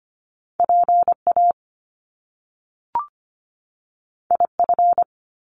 Morse code, heard as a keyed tone, about 25 wpm, 700 Hz. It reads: PA IF
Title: Top Two Letter Combinations in Words 25wpm